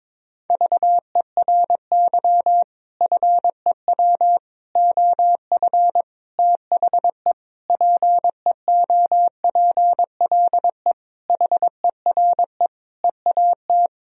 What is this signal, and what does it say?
Morse code, heard as a keyed tone, about 22 wpm, 700 Hz. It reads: VERY FEW OF THE PEOPLE HERE EAT
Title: Sentences from Top 1000 Words - Review Part 3 - 22wpm